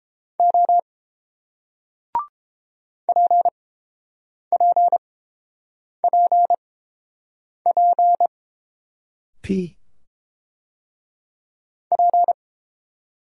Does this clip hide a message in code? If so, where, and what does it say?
Yes: Morse code from 0.4 to 12.3 s, heard as a keyed tone, about 30 wpm, spaced as for 7 wpm, 690 Hz. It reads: O PPPP P